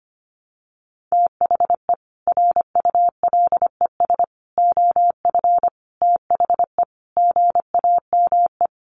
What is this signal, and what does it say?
Morse code, heard as a keyed tone, about 25 wpm, 700 Hz. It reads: THE RULES OF THE GAME
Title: Sets of 5 Words Encore2 25wpm